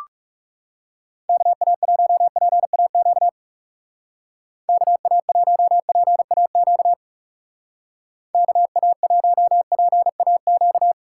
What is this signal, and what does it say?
Morse code, heard as a keyed tone, about 38 wpm, 700 Hz. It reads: KA1PAQ KA1PAQ KA1PAQ